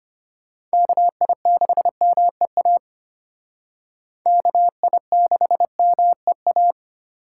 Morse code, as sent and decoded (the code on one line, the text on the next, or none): -.- .. -.... -- . .- / -.- .. -.... -- . .-
KI6MEA KI6MEA